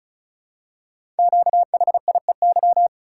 Morse code, heard as a keyed tone, about 35 wpm, 690 Hz. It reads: QHIEY
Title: Code Groups - 5 Letters - 35wpm